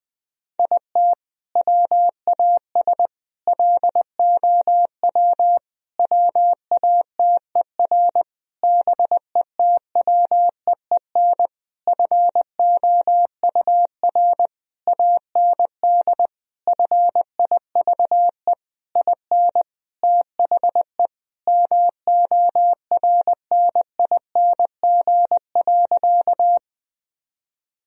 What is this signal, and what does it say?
Morse code, heard as a keyed tone, about 20 wpm, 700 Hz. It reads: IT WAS LOW WATER BETWEEN FOUR AND FIVE IN THE MORNING.
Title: Sentences from Top 1000 Words - Review Part 2 - 20wpm